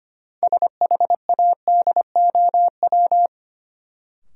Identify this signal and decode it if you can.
Morse code, heard as a keyed tone, about 25 wpm, 700 Hz. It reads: SHADOW
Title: Sets of 5 Words Encore - New Word Drill - 25wpm